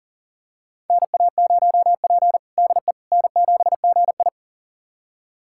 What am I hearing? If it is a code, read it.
Morse code, heard as a keyed tone, about 40 wpm, 700 Hz. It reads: NA0P DE N7GI
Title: USA Callsign DE Callsign - 4 Characters - 40wpm